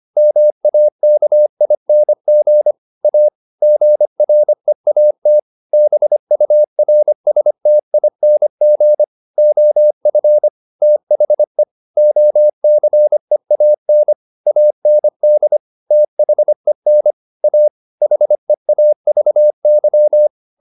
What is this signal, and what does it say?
Morse code, heard as a keyed tone, about 25 wpm, 600 Hz. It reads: MAKING A GREAT BURSTING OF THE OCEAN AND THEN A HEAVY